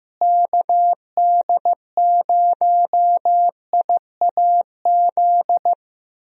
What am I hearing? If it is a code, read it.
Morse code, heard as a keyed tone, about 15 wpm, 700 Hz. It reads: KD0IAZ
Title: USA Call Signs - 6 Characters - Repeated 2x - 15wpm